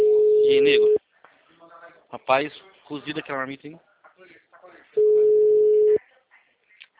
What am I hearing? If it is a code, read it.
Ringback tone (Europe, 425 Hz), with speech.
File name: out- -2004- - - .14